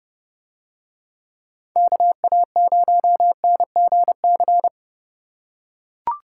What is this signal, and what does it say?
Morse code, heard as a keyed tone, about 30 wpm, 700 Hz. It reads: KA0NGC